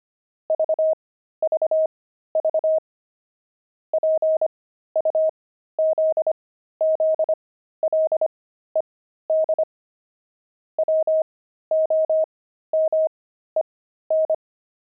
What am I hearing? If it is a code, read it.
Morse code, heard as a keyed tone, about 25 wpm, spaced as for 13 wpm, 630 Hz. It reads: VVV PUZZLED WOMEN
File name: pangram-fs-13-cs-25-jerks